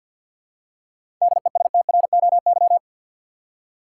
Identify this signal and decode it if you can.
Morse code, heard as a keyed tone, about 50 wpm, 700 Hz. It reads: DESTROY
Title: Seven Letter Words - 50wpm